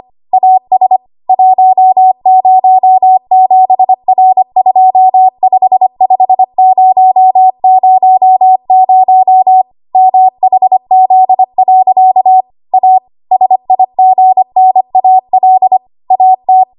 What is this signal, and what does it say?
Morse code, heard as a keyed tone, about 25 wpm, 750 Hz. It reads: AS 107R255000 MHZ. A SIGNAL AT
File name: Archive / 25 WPM / 25WPM